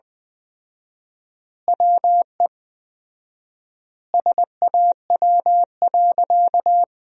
Morse code, heard as a keyed tone, about 21 wpm, 700 Hz. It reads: WE SAW.
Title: 4x Word-Spacing - Sentences from Top 300 Words - 20wpm